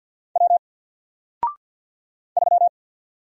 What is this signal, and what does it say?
Morse code, heard as a keyed tone, about 53 wpm, 700 Hz. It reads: W 3
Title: Single Letter-Number 50wpm